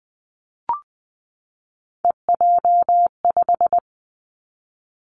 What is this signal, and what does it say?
Morse code, heard as a keyed tone, about 20 wpm, 700 Hz. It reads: EJ5
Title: Three Challenging Letter-Number Combinations 20wpm